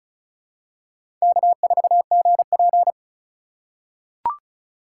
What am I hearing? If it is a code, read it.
Morse code, heard as a keyed tone, about 36 wpm, 700 Hz. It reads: K4GP
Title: USA Call Signs - 4 Characters - 4x Character Spacing - 35wpm